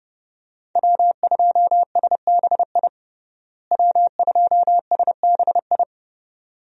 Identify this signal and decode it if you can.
Morse code, heard as a keyed tone, about 29 wpm, 700 Hz. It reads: W2SBI W2SBI
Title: USA Call Signs - 5 Characters - Repeated 3x - 30wpm